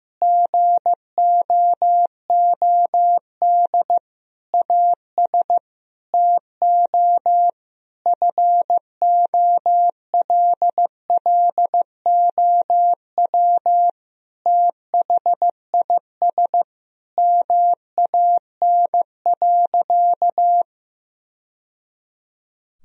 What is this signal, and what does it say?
Morse code, heard as a keyed tone, about 15 wpm, 700 Hz. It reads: GOOD AS TO FOLLOW THIS MAN.
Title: Sentences from Top 1000 Words - Review Part 1 - 15wpm